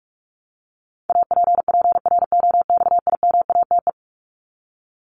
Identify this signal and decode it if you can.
Morse code, heard as a keyed tone, about 44 wpm, 700 Hz. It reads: APPROXIMATE
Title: LY Suffix - Top 100 Words with Base Words 45wpm